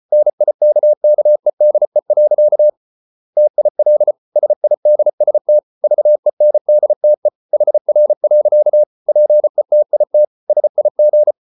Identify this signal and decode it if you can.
Morse code, heard as a keyed tone, about 34 wpm, 600 Hz. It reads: NIKKEDE. TIL SIDST VENDTE HR. PETIT SIG
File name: Ferie QRQ 34wpm